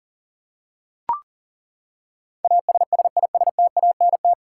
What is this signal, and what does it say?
Morse code, heard as a keyed tone, about 40 wpm, 700 Hz. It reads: ASSISTANT
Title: Practice QSO - Exchange 3 - More Info - 40wpm